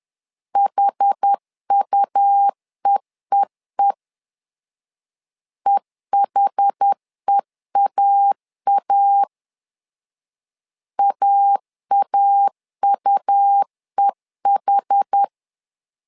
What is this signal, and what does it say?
Morse code, heard as a keyed tone, about 10 wpm, 800 Hz. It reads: HUEEE EHEAA AAUEH
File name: LEZIONI / 2 UHDN / 2 UH 10wpm 1714